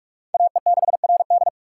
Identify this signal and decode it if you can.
Morse code, heard as a keyed tone, about 42 wpm, 700 Hz. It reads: AE6RD
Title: USA Call Signs - 5 Characters - 6x Character Spacing - 45wpm